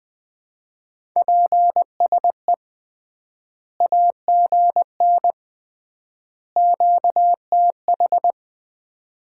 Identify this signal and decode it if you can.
Morse code, heard as a keyed tone, about 21 wpm, 700 Hz. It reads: PSE AGN QTH
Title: Common QSO Phrases - 3x Word-Spacing - 20wpm